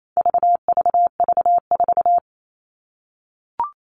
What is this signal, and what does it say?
Morse code, heard as a keyed tone, about 28 wpm, 700 Hz. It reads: VVV4
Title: All Cut Numbers - 4-Digit 28wpm